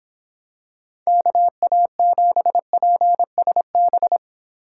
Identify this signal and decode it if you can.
Morse code, heard as a keyed tone, about 25 wpm, 700 Hz. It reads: KA7PSB